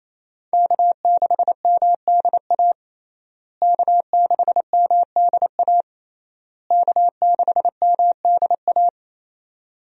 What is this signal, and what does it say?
Morse code, heard as a keyed tone, about 28 wpm, 700 Hz. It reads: K6MDA K6MDA K6MDA
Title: USA Call Signs - 5 Characters - Repeated 3x - 28wpm